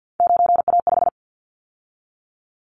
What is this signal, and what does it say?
Morse code, heard as a keyed tone, about 50 wpm, 700 Hz. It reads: 9A5
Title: Callsign Prefixes for Top 50 DXCC - 50wpm